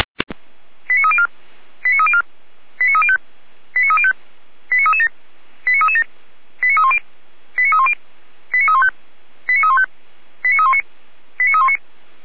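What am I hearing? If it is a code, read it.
Five-tone selective call (ZVEI), heard as quick runs of five tones, tones of about 70 ms. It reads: 89304 89304 89305 89305 89337 89337 89310 89310 89315 89315 89319 89319
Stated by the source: ZVEI five-tone call 89250 received by SDR, from the feuerwehr-orlamuende.org fire brigade server.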